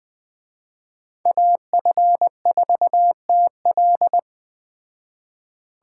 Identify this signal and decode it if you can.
Morse code, heard as a keyed tone, about 20 wpm, 700 Hz. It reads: AF4TL